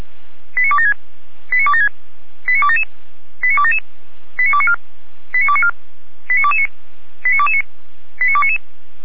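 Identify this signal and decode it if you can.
Five-tone selective call (ZVEI), heard as quick runs of five tones, tones of about 70 ms. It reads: 89276 89276 89277 89277 89294 89294 89229 89229 89299
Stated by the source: Five-tone ZVEI selective call 89265, received by SDR.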